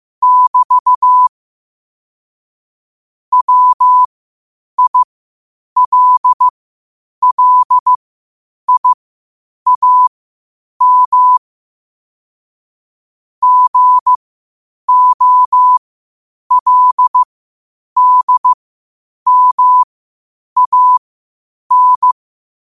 Morse code, heard as a keyed tone, about 15 wpm, spaced as for 8 wpm, 1000 Hz. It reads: <BT> WILLIAM GOLDMAN
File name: qotdfiles / 8wpm